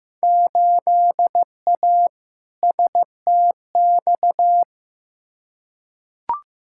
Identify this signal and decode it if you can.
Morse code, heard as a keyed tone, about 15 wpm, 700 Hz. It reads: 8A STX
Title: Field Day Exchanges Encore - 15wpm